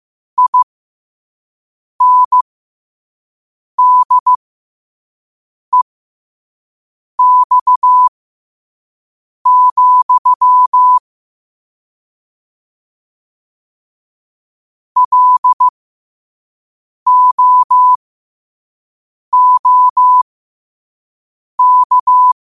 Morse code, heard as a keyed tone, about 15 wpm, spaced as for 5 wpm, 1000 Hz. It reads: INDEX, LOOK
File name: qotdfiles / 5wpm